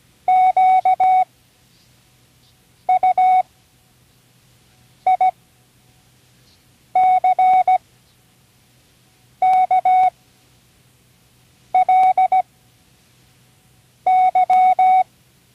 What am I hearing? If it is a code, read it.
Morse code, heard as a keyed tone, about 17 wpm, spaced as for 5 wpm, 740 Hz. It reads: QUICKLY
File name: CWResources / Learning Morse Code (from Tune in the World) / 29 words using all letters